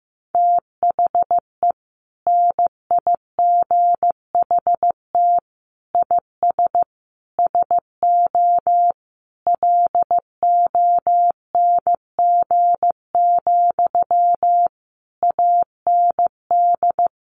Morse code, heard as a keyed tone, about 15 wpm, 700 Hz. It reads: THE NIGHT IS SO LONG, AND
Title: Sentences from Top 500 Words 15wpm